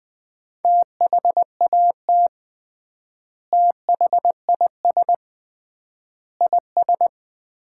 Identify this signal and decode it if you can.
Morse code, heard as a keyed tone, about 20 wpm, 700 Hz. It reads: THAT THIS IS
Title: 3x Word-Spacing - Sets of 4 Words - 20wpm